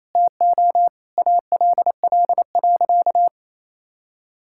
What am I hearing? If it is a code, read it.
Morse code, heard as a keyed tone, about 28 wpm, 700 Hz. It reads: TO ALL.
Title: Sentences from Top 1000 Words - Review Part 5 - 28wpm